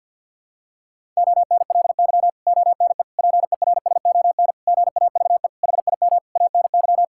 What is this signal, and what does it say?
Morse code, heard as a keyed tone, about 52 wpm, 700 Hz. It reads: ONLY ONE PERSON GAVE HIM ANY
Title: Sentences from Top 1000 Words - Review Part 6 - 50wpm